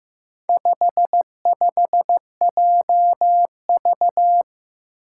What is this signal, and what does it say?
Morse code, heard as a keyed tone, about 15 wpm, 700 Hz. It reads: 55JV
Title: Four Challenging Letter-Number Combinations 15wpm